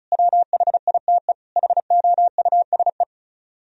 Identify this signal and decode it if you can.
Morse code, heard as a keyed tone, about 46 wpm, 700 Hz. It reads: WHITE HOUSE